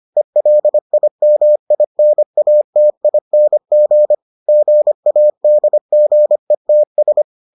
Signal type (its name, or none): Morse code